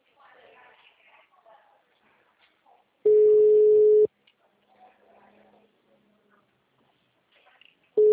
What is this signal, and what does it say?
Ringback tone (Europe, 425 Hz), with speech.
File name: out- -2013- - - .78784